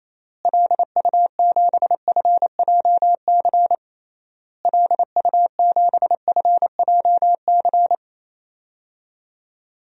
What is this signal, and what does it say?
Morse code, heard as a keyed tone, about 28 wpm, 700 Hz. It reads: LU7FJC LU7FJC
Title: DXCC-100 Call Signs - 6 Characters - Repeated 3x - 28wpm